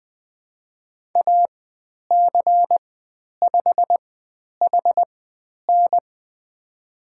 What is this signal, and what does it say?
Morse code, heard as a keyed tone, about 20 wpm, spaced as for 10 wpm, 700 Hz. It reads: AC5HN
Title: 109 Lesson 18g - Letter m - Call Signs 20wpm